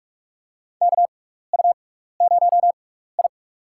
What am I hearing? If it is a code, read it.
Morse code, heard as a keyed tone, about 48 wpm, spaced as for 16 wpm, 700 Hz. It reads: KU0I
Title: USA Call Signs - 4 Characters - 6x Character Spacing - 45wpm